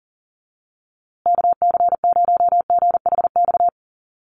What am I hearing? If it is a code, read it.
Morse code, heard as a keyed tone, about 40 wpm, 700 Hz. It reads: KC0GHX
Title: USA Call Signs Repeated 3x 40wpm